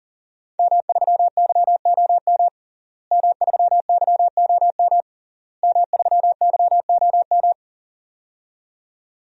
Morse code, heard as a keyed tone, about 40 wpm, 700 Hz. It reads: M3YOM M3YOM M3YOM
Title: DXCC-100 Call Signs - 5 Characters - Repeated 3x - 40wpm